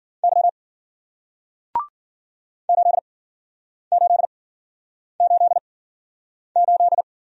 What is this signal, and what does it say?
Morse code, heard as a keyed tone, about 53 wpm, spaced as for 9 wpm, 700 Hz. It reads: Y 8888